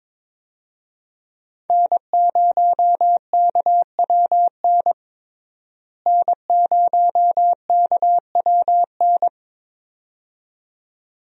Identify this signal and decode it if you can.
Morse code, heard as a keyed tone, about 22 wpm, 700 Hz. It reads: N0KWN N0KWN
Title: USA Call Signs - 5 Characters - Repeated 2x - 22wpm